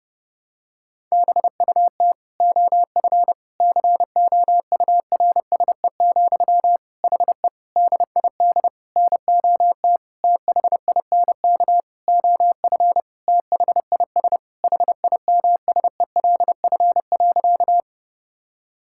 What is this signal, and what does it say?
Morse code, heard as a keyed tone, about 30 wpm, 700 Hz. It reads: BUT OF COURSE, HE DID NOT THINK OF THIS HIMSELF.
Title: Sentences from Top 1600 Words 30wpm